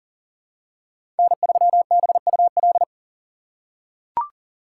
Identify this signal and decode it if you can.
Morse code, heard as a keyed tone, about 40 wpm, 700 Hz. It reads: N3BUL
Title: USA Call Signs - 2x Character Spacing - 40wpm